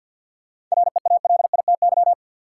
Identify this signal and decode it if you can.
Morse code, heard as a keyed tone, about 51 wpm, 700 Hz. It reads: REALITY
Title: Sentences from Top 1200 Words - New Word Drill Alt - 50wpm